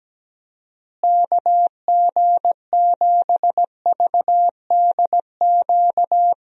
Morse code, heard as a keyed tone, about 17 wpm, 700 Hz. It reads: KG7VDQ
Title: USA Call Signs - 6 Characters - Repeated 2x - 17wpm